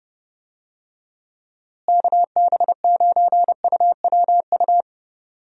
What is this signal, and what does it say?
Morse code, heard as a keyed tone, about 31 wpm, 700 Hz. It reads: KB9UWU
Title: USA Call Signs - 6 Characters - Repeated 2x - 30wpm